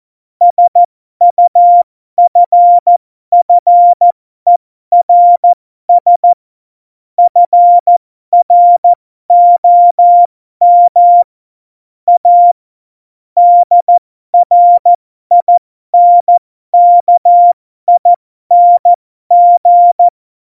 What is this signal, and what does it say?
Morse code, heard as a keyed tone, about 14 wpm, spaced as for 12 wpm, 700 Hz. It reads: SUFFERS FROM A DRINKING
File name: Morsekurs / Witze